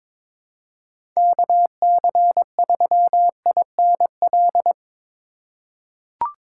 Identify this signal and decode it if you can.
Morse code, heard as a keyed tone, about 22 wpm, 700 Hz. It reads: KC3INL
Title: USA Call Signs - 6 Characters - 4x Character Spacing - 22wpm